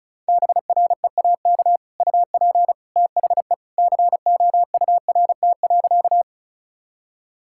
Morse code, heard as a keyed tone, about 34 wpm, 700 Hz. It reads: BREAK UP THE COURT.
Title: Sentences from Top 1600 Words 35wpm